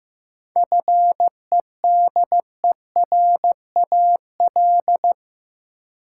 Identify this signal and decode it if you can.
Morse code, heard as a keyed tone, about 15 wpm, 700 Hz. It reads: FEDERAL
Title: Sentences from Top 1000 Words - Review Part 5 - New Word Drill - 15wpm